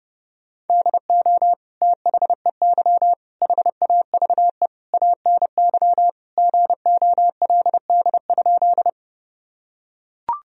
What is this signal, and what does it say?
Morse code, heard as a keyed tone, about 32 wpm, 700 Hz. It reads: DO THEY HAVE ANY GOLD?
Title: Sentences from Top 1000 Words - Review Part 3 - 30wpm